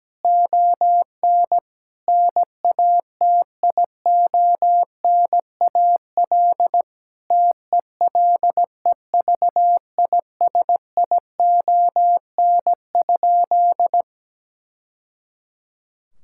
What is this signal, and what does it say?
Morse code, heard as a keyed tone, about 17 wpm, 700 Hz. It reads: ON NATIONAL TELEVISION?
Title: Sentences from Top 1000 Words - Review Part 5 - 17wpm